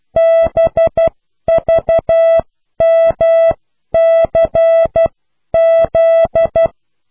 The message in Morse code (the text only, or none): BVMCZ